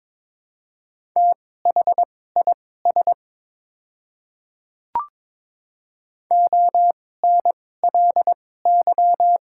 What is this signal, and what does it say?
Morse code, heard as a keyed tone, about 22 wpm, spaced as for 16 wpm, 690 Hz. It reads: THIS ONLY
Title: 2x Character Spacing - Top 200 Words - 22wpm